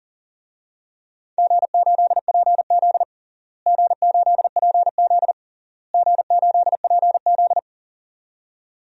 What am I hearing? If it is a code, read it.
Morse code, heard as a keyed tone, about 40 wpm, 700 Hz. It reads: G8PZ G8PZ G8PZ
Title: DXCC-100 Call Signs - 4 Characters - Repeated 3x - 40wpm